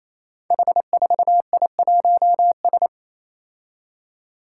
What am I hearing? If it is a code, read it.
Morse code, heard as a keyed tone, about 28 wpm, 700 Hz. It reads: H4I1S
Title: Five Challenging Letter-Number Combinations 28wpm